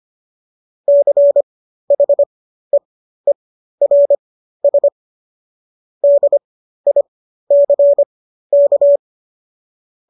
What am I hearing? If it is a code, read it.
Morse code, heard as a keyed tone, about 25 wpm, spaced as for 13 wpm, 570 Hz. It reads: CHEERS DICK